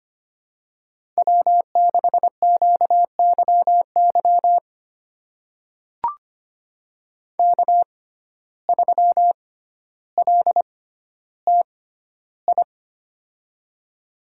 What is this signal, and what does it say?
Morse code, heard as a keyed tone, about 25 wpm, spaced as for 13 wpm, 700 Hz. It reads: W6QYY K3LTI
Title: USA Call Signs - 5 Characters - 6x Character Spacing - 25wpm